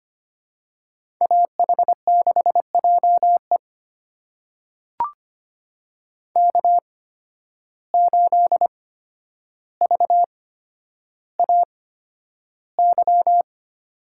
Morse code, heard as a keyed tone, about 25 wpm, spaced as for 11 wpm, 700 Hz. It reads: AH6JE K8VAY
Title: USA Call Signs - 5 Characters - 8x Character Spacing - 25wpm